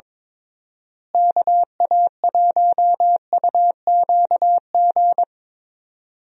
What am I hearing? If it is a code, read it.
Morse code, heard as a keyed tone, about 21 wpm, 700 Hz. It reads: KA1UQG
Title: USA Call Signs - 6 Characters - Repeated 3x - 22wpm